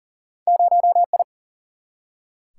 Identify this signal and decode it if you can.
Morse code, heard as a keyed tone, about 40 wpm, 700 Hz. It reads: 0I
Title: Single-Letter Single-Number 40wpm